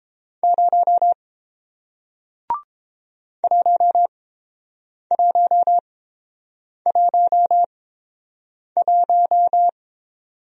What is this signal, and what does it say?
Morse code, heard as a keyed tone, about 30 wpm, spaced as for 7 wpm, 700 Hz. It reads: 0 1111